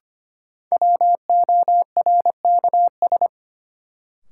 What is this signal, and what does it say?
Morse code, heard as a keyed tone, about 25 wpm, 700 Hz. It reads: WORKS
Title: Sentences from Top 1000 Words - Review Part 5 - New Word Drill - 25wpm